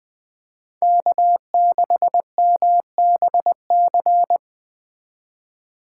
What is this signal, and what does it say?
Morse code, heard as a keyed tone, about 20 wpm, 700 Hz. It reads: K6MBC